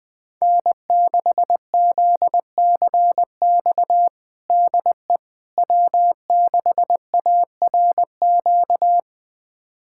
Morse code, heard as a keyed tone, about 20 wpm, 700 Hz. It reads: N6ZCX DE W6ARQ
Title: USA Callsign DE Callsign - 5 Characters - 20wpm